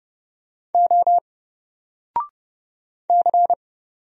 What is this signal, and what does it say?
Morse code, heard as a keyed tone, about 29 wpm, 700 Hz. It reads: O C
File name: Single-Letter-Number-30wpm